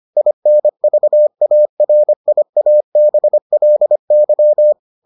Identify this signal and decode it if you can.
Morse code, heard as a keyed tone, about 25 wpm, 600 Hz. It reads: INVARIABLY